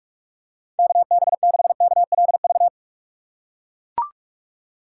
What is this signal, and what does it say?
Morse code, heard as a keyed tone, about 44 wpm, 700 Hz. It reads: KB6KLV